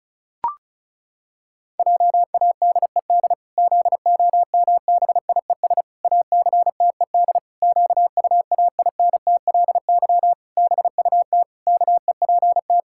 Morse code, heard as a keyed tone, about 34 wpm, 700 Hz. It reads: JADED ZOMBIES ACTED QUAINTLY BUT KEPT
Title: Pangram Challenge v2 - 35wpm